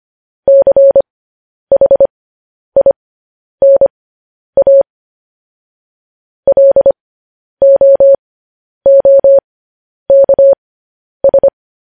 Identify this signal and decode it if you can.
Morse code, heard as a keyed tone, about 25 wpm, spaced as for 10 wpm, 550 Hz. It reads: CHINA LOOKS
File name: podcast / News headlines-1x-25x10